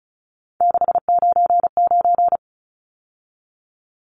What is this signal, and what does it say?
Morse code, heard as a keyed tone, about 34 wpm, 700 Hz. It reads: 699
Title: Numbers - 3-Digit 35wpm